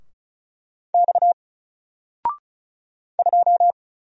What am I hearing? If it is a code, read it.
Morse code, heard as a keyed tone, about 36 wpm, 700 Hz. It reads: X 2
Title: Single Character (Letter:Number:Punctuation:Prosign) 35wpm